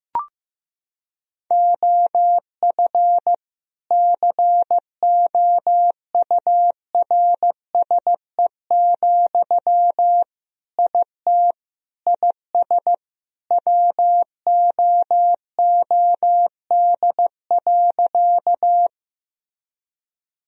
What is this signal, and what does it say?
Morse code, heard as a keyed tone, about 16 wpm, 700 Hz. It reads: OF COURSE, IT IS WOOD.
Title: Sentences from Top 1000 Words - Review Part 2 - 15wpm